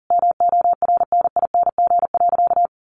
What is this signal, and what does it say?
Morse code, heard as a keyed tone, about 40 wpm, 700 Hz. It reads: MORNING.
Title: Sentences from Top 1500 Words 40wpm